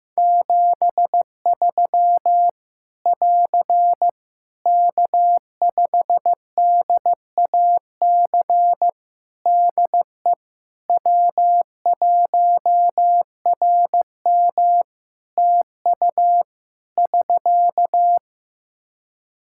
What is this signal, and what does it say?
Morse code, heard as a keyed tone, about 15 wpm, 700 Hz. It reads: 73 <AR> K5DAC DE W1RM TU <SK>
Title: Practice QSO - Exchange 4 - Ending - 15wpm